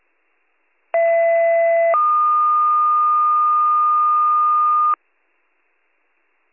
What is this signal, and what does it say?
Two-tone page, heard as a one-second tone, then a three-second tone, about 670 then 1220 Hz. It is heard from 0.9 to 4.9 s.